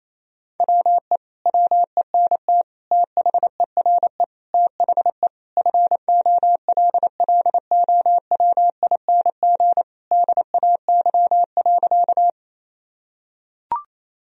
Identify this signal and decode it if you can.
Morse code, heard as a keyed tone, about 28 wpm, 700 Hz. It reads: WE WENT THERE THE FOLLOWING DAY.
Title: Sentences from Top 1000 Words - Review Part 4 - 28wpm